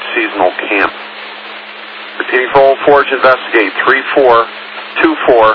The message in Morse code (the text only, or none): none